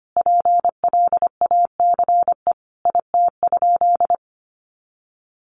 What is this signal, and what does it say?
Morse code, heard as a keyed tone, about 25 wpm, 700 Hz. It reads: PLACE IT?
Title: Sentences from Top 200 Words 25wpm